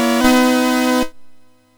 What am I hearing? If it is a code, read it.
There is no code.